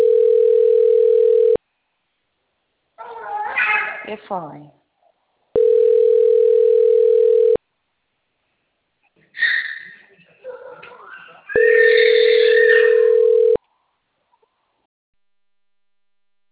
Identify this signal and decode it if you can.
Ringback tone (North America), 440 + 480 Hz, with speech.
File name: rg-1600-unknown- - - .18507